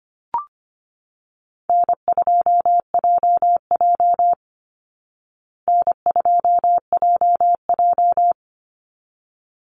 Morse code, heard as a keyed tone, about 25 wpm, 700 Hz. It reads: N2JJ N2JJ
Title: Easy Call Signs Repeated 2x - 25wpm